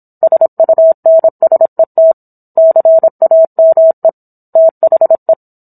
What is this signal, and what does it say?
Morse code, heard as a keyed tone, about 27 wpm, 650 Hz. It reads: SUNSET CAME THE
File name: Book One / 17-Chapter 17 - 26 WPM